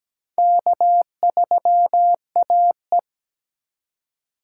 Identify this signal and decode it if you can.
Morse code, heard as a keyed tone, about 17 wpm, 700 Hz. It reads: K3AE
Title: USA Call Signs - 4 Characters - 4x Character Spacing - 17wpm